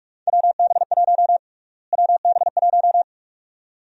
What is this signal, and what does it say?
Morse code, heard as a keyed tone, about 45 wpm, 700 Hz. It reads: WB1 WB1
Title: USA Call Sign Prefixes - Repeated 2x - 45wpm